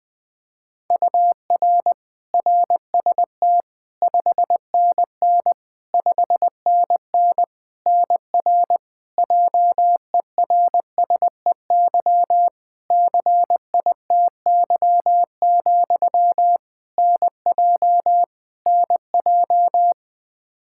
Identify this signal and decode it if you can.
Morse code, heard as a keyed tone, about 20 wpm, 700 Hz. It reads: UR RST 5NN 5NN NR JERSEY CITY, NJ NJ
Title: Practice QSO - Exchange 1 - Beginning - 20wpm